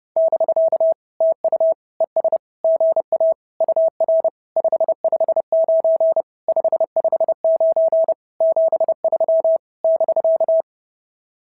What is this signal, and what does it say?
Morse code, heard as a keyed tone, about 28 wpm, 650 Hz. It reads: <BK> TU ES GA UR 559 559 73 <BK>
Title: SOTA QSO Practice 30wpm